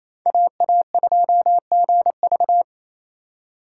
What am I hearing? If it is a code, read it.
Morse code, heard as a keyed tone, about 27 wpm, 700 Hz. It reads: AA2GV
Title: USA Call Signs - 5 Characters - 4x Character Spacing - 28wpm